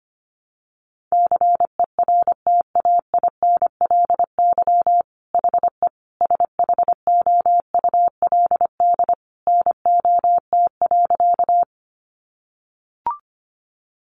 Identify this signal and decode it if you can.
Morse code, heard as a keyed tone, about 25 wpm, 700 Hz. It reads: CERTAINLY HE SHOULD NOT.
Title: Sentences from Top 900 Words 25wpm